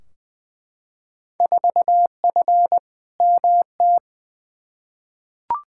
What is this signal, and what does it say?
Morse code, heard as a keyed tone, about 20 wpm, 700 Hz. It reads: 4F MT
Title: Field Day Exchanges Encore - 20wpm